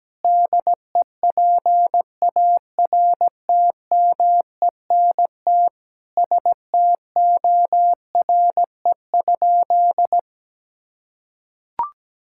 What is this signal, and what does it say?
Morse code, heard as a keyed tone, about 17 wpm, 700 Hz. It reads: DEPARTMENT STORE?
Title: Sentences from Top 1000 Words - Review Part 5 - 17wpm